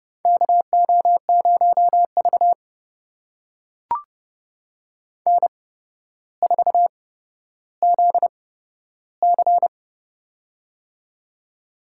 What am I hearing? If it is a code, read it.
Morse code, heard as a keyed tone, about 29 wpm, spaced as for 13 wpm, 700 Hz. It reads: KO0V N4ZC
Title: USA Call Signs - 4 Characters - 8x Character Spacing - 30wpm